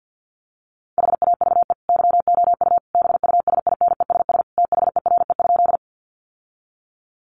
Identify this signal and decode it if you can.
Morse code, heard as a keyed tone, about 50 wpm, 700 Hz. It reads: HAVE YOU BUSINESS THERE?
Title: Sentences from Top 600 Words 50wpm